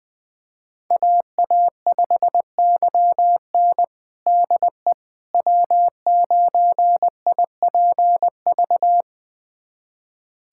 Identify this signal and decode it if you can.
Morse code, heard as a keyed tone, about 21 wpm, 700 Hz. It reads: AA5YN DE W9IPV
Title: USA Callsign DE Callsign - 5 Characters - 20wpm